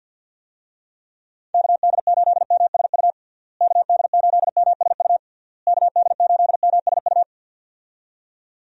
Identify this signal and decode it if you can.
Morse code, heard as a keyed tone, about 52 wpm, 700 Hz. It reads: KD8MSU KD8MSU KD8MSU
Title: USA Call Signs - 6 Characters - Repeated 3x - 50wpm